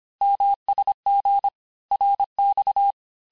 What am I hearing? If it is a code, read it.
Morse code, heard as a keyed tone, about 26 wpm, 800 Hz. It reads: MSG RX
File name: cours-du-lundi files / cours-1584-texte-abreviations-vitesse-19mpm-puis-24mpm